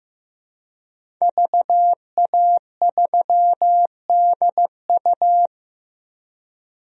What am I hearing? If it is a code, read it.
Morse code, heard as a keyed tone, about 15 wpm, 700 Hz. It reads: VA3DU
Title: DXCC-100 Call Signs - 5 Characters - 15wpm